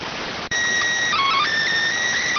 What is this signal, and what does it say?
SSTV (slow-scan television), PD 290.